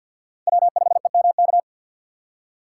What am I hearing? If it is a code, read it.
Morse code, heard as a keyed tone, about 51 wpm, 700 Hz. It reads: W5EMK
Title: USA Call Signs - 5 Characters - 2x Character Spacing - 50wpm